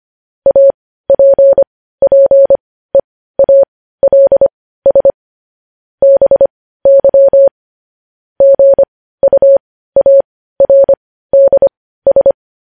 Morse code, heard as a keyed tone, about 25 wpm, spaced as for 15 wpm, 550 Hz. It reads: APPEALS BY GUARDS